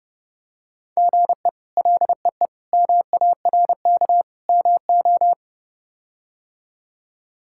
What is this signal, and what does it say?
Morse code, heard as a keyed tone, about 30 wpm, 700 Hz. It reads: GE LEE MARK MO
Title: K1USN SST QSO Practice - 30wpm